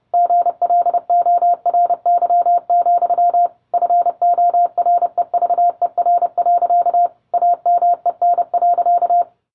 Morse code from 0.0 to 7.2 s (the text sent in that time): GLORY, FOREVER.